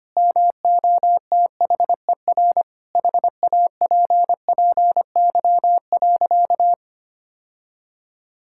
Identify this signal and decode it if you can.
Morse code, heard as a keyed tone, about 25 wpm, 700 Hz. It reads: MOTHER HAPPY.
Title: Sentences from Top 1000 Words - Review Part 5 - 25wpm